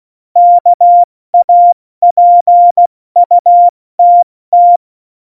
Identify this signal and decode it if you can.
Morse code, heard as a keyed tone, about 16 wpm, 700 Hz. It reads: KAPUTT
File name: Morsekurs / Witze / Witz 29. Januar 2019, 70 F80